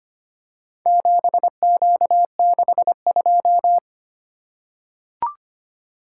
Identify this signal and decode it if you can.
Morse code, heard as a keyed tone, about 25 wpm, 700 Hz. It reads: 7Q62
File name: practice / 25wpm / Four Letter-Number Combinations - 25wpm